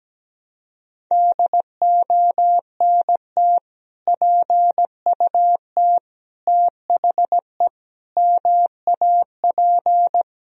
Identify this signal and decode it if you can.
Morse code, heard as a keyed tone, about 17 wpm, 700 Hz. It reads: DONT PUT THE MAP
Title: Sentences from Top 1000 Words - Review Part 2 - 17wpm